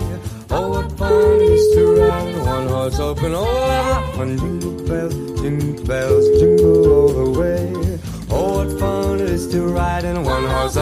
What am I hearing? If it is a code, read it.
Ringback tone (Europe, 425 Hz), with noise.